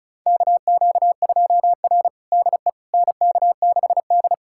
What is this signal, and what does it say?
Morse code, heard as a keyed tone, about 34 wpm, 700 Hz. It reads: KQ2R DE NK6D
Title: USA Callsign DE Callsign - 4 Characters - 35wpm